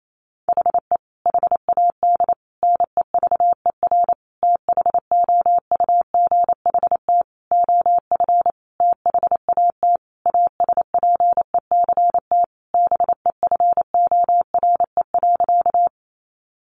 Morse code, heard as a keyed tone, about 28 wpm, 700 Hz. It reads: HE HAD NEVER THOUGHT OF THAT ASPECT BEFORE.
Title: Sentences from Top 1300 Words 28wpm